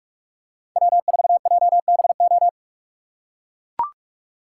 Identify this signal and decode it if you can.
Morse code, heard as a keyed tone, about 44 wpm, 700 Hz. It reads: W4JBO